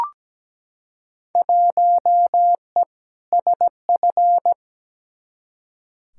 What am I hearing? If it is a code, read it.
Morse code, heard as a keyed tone, about 17 wpm, 700 Hz. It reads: 1E SF
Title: Field Day Exchanges - 17wpm